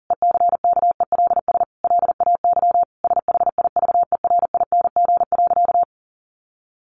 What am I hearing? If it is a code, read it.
Morse code, heard as a keyed tone, about 37 wpm, 700 Hz. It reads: ECKELS LAY SHIVERING.
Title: A Sound Of Thunder 40wpm